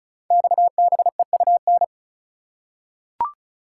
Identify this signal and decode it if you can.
Morse code, heard as a keyed tone, about 36 wpm, 700 Hz. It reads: XBEUN
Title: Code Groups - 5 Letters - 35wpm